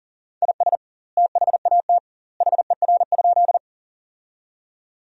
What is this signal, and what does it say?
Morse code, heard as a keyed tone, about 40 wpm, 700 Hz. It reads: IS THAT HER?
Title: 2x Word-Spacing - Sentences from Top 100 Words - 40wpm